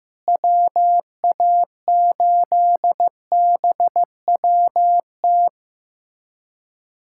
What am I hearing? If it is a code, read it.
Morse code, heard as a keyed tone, about 15 wpm, 700 Hz. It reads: WA8BWT
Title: USA Callsign DE Callsign 15wpm